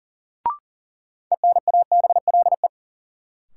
Morse code, heard as a keyed tone, about 40 wpm, 700 Hz. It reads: ENABLE
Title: Sentences from Top 1500 Words - New Word Drill Alt - 40wpm